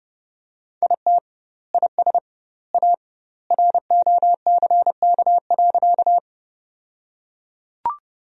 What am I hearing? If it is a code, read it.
Morse code, heard as a keyed tone, about 32 wpm, 700 Hz. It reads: IT IS A ROCK.
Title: 2x Word-Spacing - Sentences from Top 400 Words - 30wpm